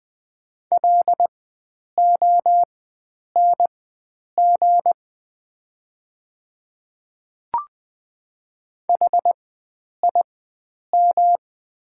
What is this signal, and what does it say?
Morse code, heard as a keyed tone, about 20 wpm, spaced as for 9 wpm, 700 Hz. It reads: LONG HIM
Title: 4x Character Spacing - Top 100 Words - 20wpm